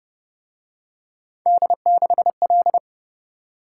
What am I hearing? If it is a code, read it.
Morse code, heard as a keyed tone, about 30 wpm, 700 Hz. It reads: D6L